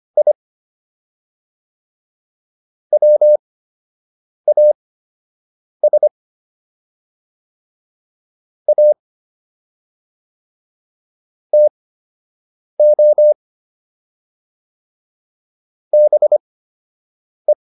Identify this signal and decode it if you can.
Morse code, heard as a keyed tone, about 25 wpm, spaced as for 7 wpm, 600 Hz. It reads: I WAS A TO BE